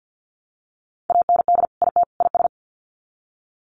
Morse code, heard as a keyed tone, about 50 wpm, 700 Hz. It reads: AND IT IS
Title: Sets of 3 Words 50wpm